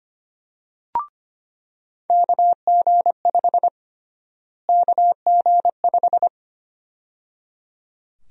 Morse code, heard as a keyed tone, about 25 wpm, 700 Hz. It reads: KG5 KG5
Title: USA Call Sign Prefixes - Repeated 2x - 25wpm